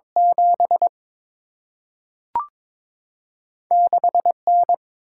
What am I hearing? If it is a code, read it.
Morse code, heard as a keyed tone, about 22 wpm, 700 Hz. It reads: 7 6 N
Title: Single-Letter Single-Number 22wpm